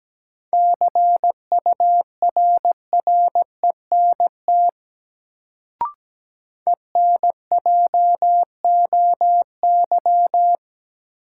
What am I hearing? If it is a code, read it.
Morse code, heard as a keyed tone, about 17 wpm, 700 Hz. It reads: CURRENT ENJOY
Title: Sentences from Top 1000 Words - Review Part 4 - New Word Drill - 17wpm